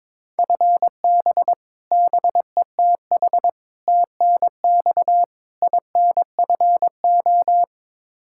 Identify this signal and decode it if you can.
Morse code, heard as a keyed tone, about 22 wpm, 700 Hz. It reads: FB BETH TNX INFO
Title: Practice QSO - Exchange 3 - More Info - 22wpm